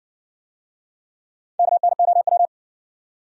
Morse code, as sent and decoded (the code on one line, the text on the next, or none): -.- -. --- .--
KNOW